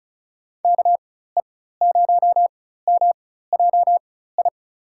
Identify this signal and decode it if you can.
Morse code, heard as a keyed tone, about 34 wpm, spaced as for 16 wpm, 700 Hz. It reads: KE0MJI
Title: USA Call Signs - 6 Characters - 4x Character Spacing - 35wpm